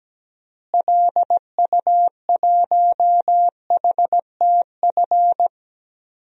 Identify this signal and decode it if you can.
Morse code, heard as a keyed tone, about 17 wpm, 700 Hz. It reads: LU1HTF
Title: DXCC-100 Call Signs - 6 Characters - Repeated 2x - 17wpm